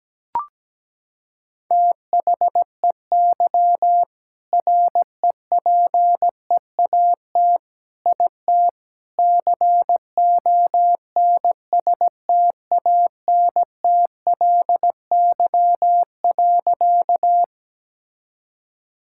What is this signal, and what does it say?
Morse code, heard as a keyed tone, about 17 wpm, 700 Hz. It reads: THEY REPEAT IT CONSTANTLY.
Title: Sentences from Top 1600 Words 17wpm